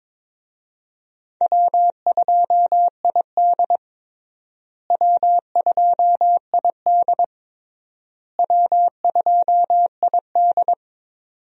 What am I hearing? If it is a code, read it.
Morse code, heard as a keyed tone, about 22 wpm, 700 Hz. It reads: W2ID W2ID W2ID
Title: USA Call Signs - 4 Characters - Repeated 3x - 22wpm